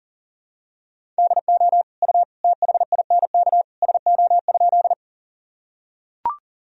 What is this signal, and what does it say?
Morse code, heard as a keyed tone, about 40 wpm, 700 Hz. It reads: DO U THINK SO?
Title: Sentences from Top 200 Words and Abbreviations 40wpm